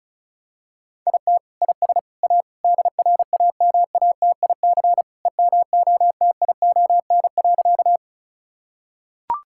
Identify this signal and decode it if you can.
Morse code, heard as a keyed tone, about 36 wpm, 700 Hz. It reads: IT IS A DRAMATIC EMOTION.
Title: Sentences from Top 1600 Words 35wpm